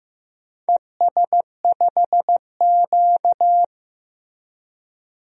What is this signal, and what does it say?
Morse code, heard as a keyed tone, about 15 wpm, 700 Hz. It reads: ES5Q
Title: DXCC-100 Call Signs - 4 Characters - Repeated 2x - 15wpm